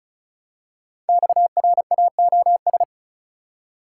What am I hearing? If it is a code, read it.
Morse code, heard as a keyed tone, about 35 wpm, 700 Hz. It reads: XRAOS
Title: Code Groups - 5 Letters - 35wpm